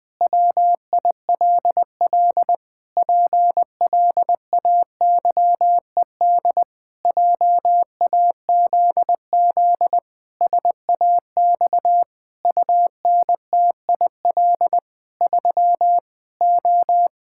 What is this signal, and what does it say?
Morse code, heard as a keyed tone, about 20 wpm, 700 Hz. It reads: WILL PLAYED JAZZ SAX UNTIL 3 O
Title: Pangram Challenge v2 - 20wpm